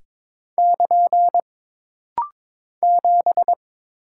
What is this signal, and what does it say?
Morse code, heard as a keyed tone, about 22 wpm, 700 Hz. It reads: <KN> 7
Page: This rapid-fire timing is geared to push your ICR (Instant Character Recognition) skills.